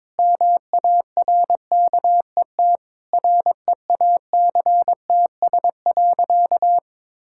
Morse code, heard as a keyed tone, about 22 wpm, 700 Hz. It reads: MARKET REACTS.
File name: practice / 22wpm / Sentences with 8-Word Phrases - 22wpm